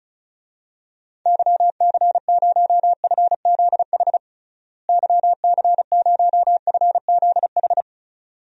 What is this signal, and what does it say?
Morse code, heard as a keyed tone, about 34 wpm, 700 Hz. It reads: YC0FZH YC0FZH
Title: DXCC-100 Call Signs - 6 Characters - Repeated 2x - 35wpm